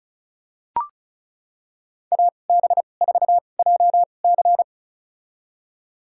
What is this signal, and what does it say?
Morse code, heard as a keyed tone, about 36 wpm, spaced as for 25 wpm, 700 Hz. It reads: AB4JC
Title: USA Call Signs - 2x Character Spacing - 35wpm